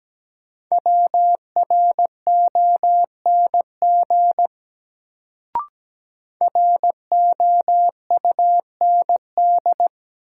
Morse code, heard as a keyed tone, about 17 wpm, 700 Hz. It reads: WRONG ROUND
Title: Sets of 5 Words Encore - New Word Drill - 17wpm